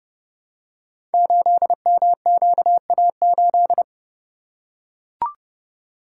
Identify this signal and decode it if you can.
Morse code, heard as a keyed tone, about 28 wpm, 700 Hz. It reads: 8MQA8
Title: Code Groups - 5 Letter:Numbers - 30wpm